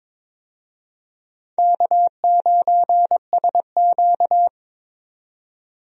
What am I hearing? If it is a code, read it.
Morse code, heard as a keyed tone, about 22 wpm, 700 Hz. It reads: K9SQ
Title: USA Call Signs - 4 Characters - 22wpm